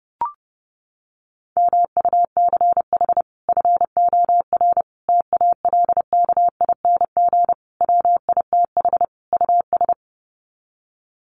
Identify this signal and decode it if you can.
Morse code, heard as a keyed tone, about 30 wpm, 700 Hz. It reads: MUCH FOR TALKING WITH US
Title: Sets of 5 Words Encore 30wpm